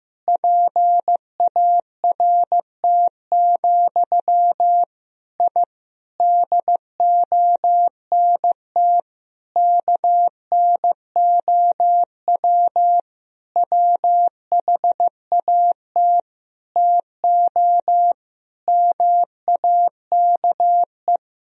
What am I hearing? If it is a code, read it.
Morse code, heard as a keyed tone, about 15 wpm, 700 Hz. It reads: PART, I DONT KNOW WHAT TO MAKE
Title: Sentences from Top 1000 Words - Review Part 1 - 15wpm